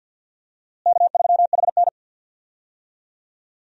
Morse code, heard as a keyed tone, about 48 wpm, 700 Hz. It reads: K3HN
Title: USA Call Signs - 4 Characters - Repeated 3x - 50wpm